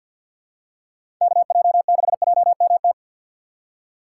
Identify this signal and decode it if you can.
Morse code, heard as a keyed tone, about 53 wpm, 700 Hz. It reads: KJ6JMT
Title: USA Call Signs - 6 Characters - 3x Character Spacing - 50wpm